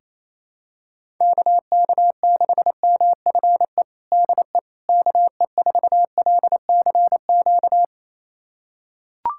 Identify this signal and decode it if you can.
Morse code, heard as a keyed tone, about 28 wpm, 700 Hz. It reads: KK6MFE DE KE4LCQ
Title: USA Callsign DE Callsign - 6 Characters - 28wpm